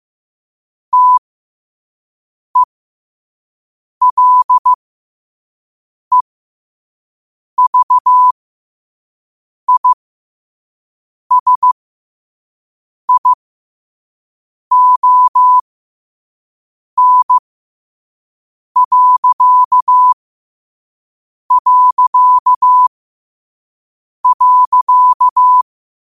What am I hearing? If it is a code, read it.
Morse code, heard as a keyed tone, about 15 wpm, spaced as for 5 wpm, 1000 Hz. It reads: TELEVISION...
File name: qotdfiles / 5wpm